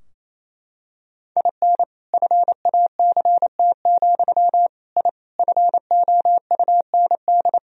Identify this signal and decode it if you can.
Morse code, heard as a keyed tone, about 28 wpm, 700 Hz. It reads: IN FACT, I FOUND
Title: Sentences from Top 1000 Words - Review Part 3 - 28wpm